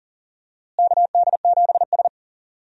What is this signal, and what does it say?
Morse code, heard as a keyed tone, about 40 wpm, 700 Hz. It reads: KD7S
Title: USA Call Signs - 4 Characters - Repeated 3x - 40wpm